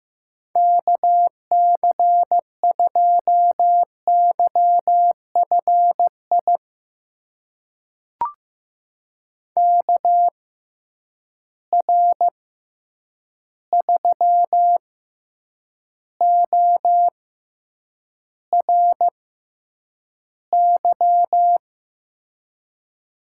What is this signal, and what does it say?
Morse code, heard as a keyed tone, about 15 wpm, spaced as for 8 wpm, 700 Hz. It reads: KC2YFI KR3ORY
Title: USA Call Signs - 6 Characters - 6x Character Spacing - 15wpm